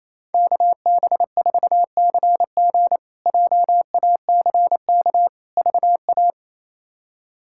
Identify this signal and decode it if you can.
Morse code, heard as a keyed tone, about 28 wpm, 700 Hz. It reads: KB4CG JACK VA
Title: K1USN SST QSO Practice - 28wpm